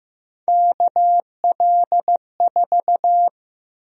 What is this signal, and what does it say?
Morse code, heard as a keyed tone, about 15 wpm, 700 Hz. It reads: KL4
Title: USA Call Sign Prefixes - Repeated 2x - 15wpm